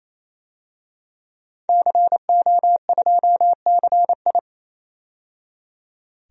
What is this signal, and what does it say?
Morse code, heard as a keyed tone, about 29 wpm, 690 Hz. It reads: CO2CI
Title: DXCC-100 Call Signs - 5 Characters - 28wpm